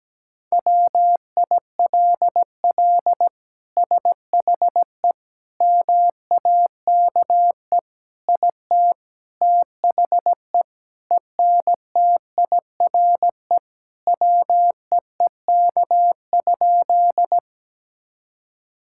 Morse code, heard as a keyed tone, about 17 wpm, 700 Hz. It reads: WILL SHE MAKE IT THE ENTIRE WEEK?
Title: Sentences from Top 1000 Words - Review Part 5 - 17wpm